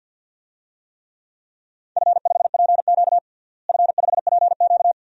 Morse code, heard as a keyed tone, about 50 wpm, 700 Hz. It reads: F5PQ F5PQ
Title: DXCC-100 Call Signs - 4 Characters - Repeated 2x - 50wpm